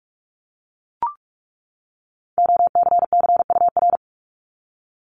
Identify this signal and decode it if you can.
Morse code, heard as a keyed tone, about 44 wpm, 700 Hz. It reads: OCCUR
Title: Top 751-1000 Words 45wpm